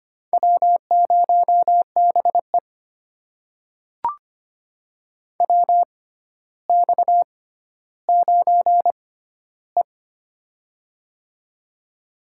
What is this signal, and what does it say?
Morse code, heard as a keyed tone, about 25 wpm, spaced as for 13 wpm, 700 Hz. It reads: W0BE WX9E
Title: USA Call Signs - 4 Characters - 6x Character Spacing - 25wpm